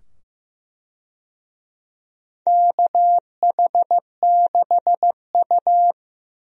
Morse code, heard as a keyed tone, about 15 wpm, 700 Hz. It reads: KH6U
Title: DXCC-100 Call Signs - 4 Characters - 15wpm